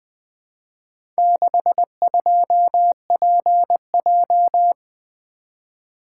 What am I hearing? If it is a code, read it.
Morse code, heard as a keyed tone, about 20 wpm, 700 Hz. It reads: 62PJ
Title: Four Letter-Number Combinations - 20wpm